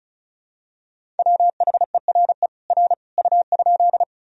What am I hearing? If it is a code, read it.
Morse code, heard as a keyed tone, about 36 wpm, 700 Hz. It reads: WHERE R U?
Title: Sentences from Top 200 Words and Abbreviations 35wpm